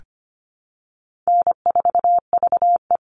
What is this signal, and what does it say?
Morse code, heard as a keyed tone, about 25 wpm, 700 Hz. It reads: N4VE
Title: All Cut Numbers - 4-Digit 25wpm